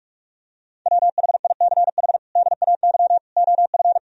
Morse code, heard as a keyed tone, about 46 wpm, 700 Hz. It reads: WHICH DAY OF
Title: Sentences from Top 1000 Words - Review Part 3 - 45wpm